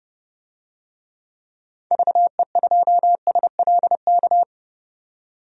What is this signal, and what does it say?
Morse code, heard as a keyed tone, about 30 wpm, 700 Hz. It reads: VE2SLK
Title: DXCC-100 Call Signs - 6 Characters - 30wpm